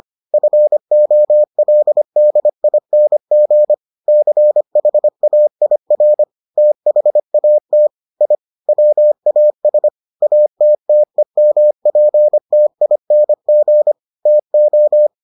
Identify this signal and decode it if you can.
Morse code, heard as a keyed tone, about 25 wpm, 600 Hz. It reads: FOLDING CHAIR THAT I WAS ATTEMPTING TO